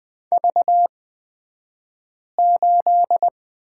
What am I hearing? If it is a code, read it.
Morse code, heard as a keyed tone, about 20 wpm, spaced as for 5 wpm, 700 Hz. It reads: V8
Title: 242 Lesson 39d - Letter z - Two Characters 20wpm